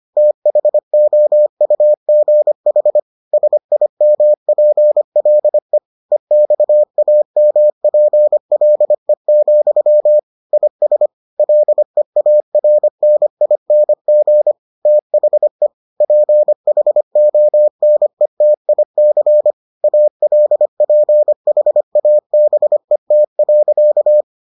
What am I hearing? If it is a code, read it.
Morse code, heard as a keyed tone, about 25 wpm, 600 Hz. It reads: THOUGH SIMPLE EXAMPLE, IS LEARNING THE PHONETIC ALPHABET.